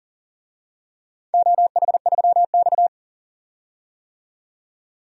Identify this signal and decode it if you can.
Morse code, heard as a keyed tone, about 38 wpm, 700 Hz. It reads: OH3X
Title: DXCC-100 Call Signs - 4 Characters - 40wpm